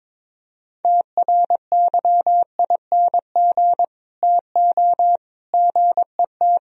Morse code, heard as a keyed tone, about 24 wpm, 700 Hz. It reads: TRYING TO GET
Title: Speed-Racing - Sets of 3 Words 22wpm